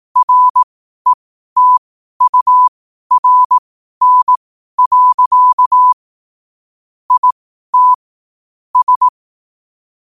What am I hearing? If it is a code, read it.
Morse code, heard as a keyed tone, about 18 wpm, spaced as for 12 wpm, 1000 Hz. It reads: RETURN. IT S
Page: qotd-at-12-wpm